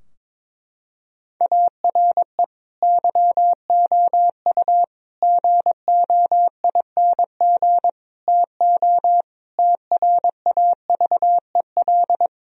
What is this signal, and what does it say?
Morse code, heard as a keyed tone, about 22 wpm, 700 Hz. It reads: ARE YOU GOING TO TRAVEL
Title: Sentences from Top 1000 Words - Review Part 3 - 22wpm